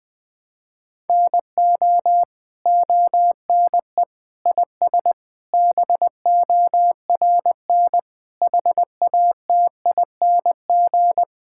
Morse code, heard as a keyed tone, about 20 wpm, 700 Hz. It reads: NO ONE IS BORN HATING
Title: Sentences with 9-Word Phrases - 20wpm